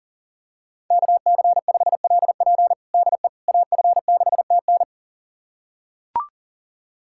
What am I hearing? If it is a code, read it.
Morse code, heard as a keyed tone, about 40 wpm, 700 Hz. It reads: KC5LP DE AF6TN
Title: USA Callsign DE Callsign - 5 Characters - 40wpm